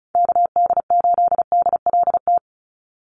Morse code, heard as a keyed tone, about 34 wpm, 700 Hz. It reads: KD8DLT
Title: USA Call Signs 35wpm